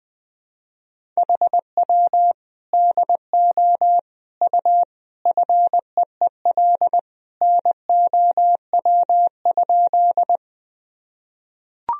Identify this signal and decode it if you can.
Morse code, heard as a keyed tone, about 20 wpm, 700 Hz. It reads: HW DO U FEEL NOW?
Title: Sentences from Top 400 Words and Abbreviations 20wpm